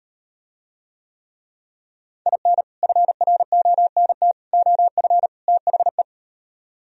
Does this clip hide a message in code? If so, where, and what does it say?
Yes: Morse code from 2.3 to 6.0 s, heard as a keyed tone, about 38 wpm, 700 Hz. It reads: IN FRONT OF THE